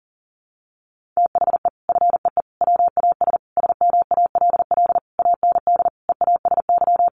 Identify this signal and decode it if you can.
Morse code, heard as a keyed tone, about 40 wpm, 700 Hz. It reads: THE FEE WAS SMALL AND EASY
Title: Sentences from Top 1500 Words 40wpm